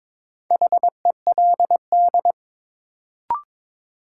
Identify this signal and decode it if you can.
Morse code, heard as a keyed tone, about 22 wpm, 700 Hz. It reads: HELD
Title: Sentences from Top 1600 Words - New Word Drill Alt - 22wpm